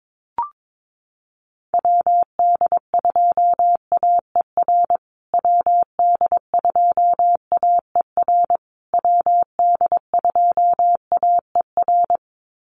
Morse code, heard as a keyed tone, about 22 wpm, 700 Hz. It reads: WD2AER WD2AER WD2AER
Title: USA Call Signs Repeated 3x 22wpm